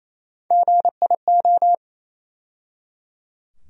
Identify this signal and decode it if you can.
Morse code, heard as a keyed tone, about 28 wpm, 700 Hz. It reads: GIO
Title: USA Call Sign Postfixes - 3 Letters 28wpm